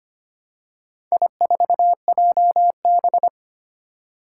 Morse code, heard as a keyed tone, about 25 wpm, 700 Hz. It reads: I4JB
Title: Four Challenging Letter-Number Combinations 25wpm